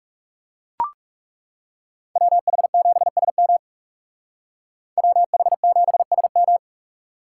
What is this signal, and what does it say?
Morse code, heard as a keyed tone, about 38 wpm, 700 Hz. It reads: WH7SM WH7SM